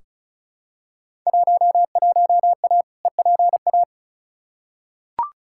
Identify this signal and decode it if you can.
Morse code, heard as a keyed tone, about 36 wpm, 690 Hz. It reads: 11A EPA
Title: Field Day Exchanges Encore - 35wpm